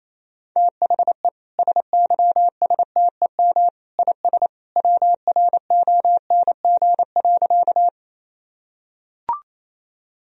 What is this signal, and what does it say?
Morse code, heard as a keyed tone, about 28 wpm, 700 Hz. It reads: THE SYSTEM IS WRONG.
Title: Sentences from Top 1000 Words - Review Part 3 - 28wpm